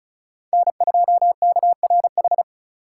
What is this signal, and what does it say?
Morse code, heard as a keyed tone, about 36 wpm, 700 Hz. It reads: N2KRH
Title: USA Call Signs - 5 Characters - 3x Character Spacing - 35wpm